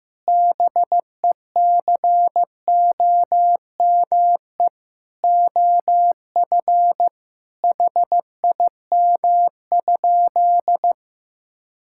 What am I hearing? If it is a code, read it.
Morse code, heard as a keyed tone, about 14 wpm, 700 Hz. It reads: BECOME OF HIM?
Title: Sentences from Top 1000 Words - Review Part 5 - 15wpm